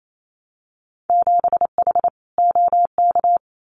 Morse code, heard as a keyed tone, about 28 wpm, 700 Hz. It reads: 7H OK
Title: Winter Field Day Exchanges Encore 28wpm